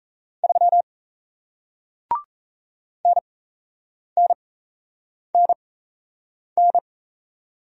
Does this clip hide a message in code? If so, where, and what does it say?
Yes: Morse code from 0.4 to 6.8 s, heard as a keyed tone, about 38 wpm, spaced as for 8 wpm, 700 Hz. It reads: 3 NNNN